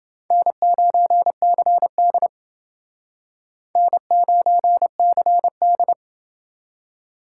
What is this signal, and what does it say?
Morse code, heard as a keyed tone, about 29 wpm, 700 Hz. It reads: N9CD N9CD